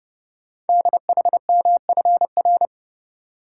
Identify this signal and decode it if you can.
Morse code, heard as a keyed tone, about 28 wpm, 700 Hz. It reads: DHMFR
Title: Code Groups - 5 Letters - 30wpm